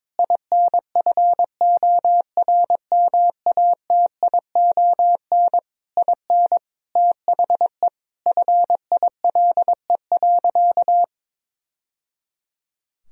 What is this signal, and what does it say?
Morse code, heard as a keyed tone, about 21 wpm, 700 Hz. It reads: INFORMATION IN THE FILE.
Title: Sentences from Top 1000 Words - Review Part 4 - 22wpm